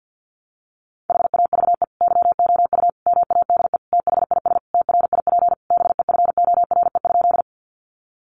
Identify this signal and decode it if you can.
Morse code, heard as a keyed tone, about 48 wpm, 700 Hz. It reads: HAVE YOU MADE THIS TRIP BEFORE?
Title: Sentences from Top 1300 Words 50wpm